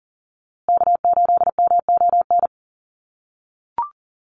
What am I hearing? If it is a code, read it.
Morse code, heard as a keyed tone, about 37 wpm, 700 Hz. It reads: K8MON
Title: USA Call Signs Repeated 2x 40wpm